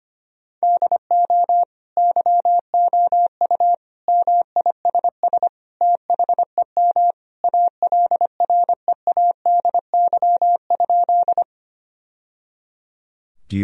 Morse code, heard as a keyed tone, about 25 wpm, 700 Hz. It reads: DO YOU MISS THEM ALREADY?
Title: Sentences from Top 1000 Words - Review Part 3 - 25wpm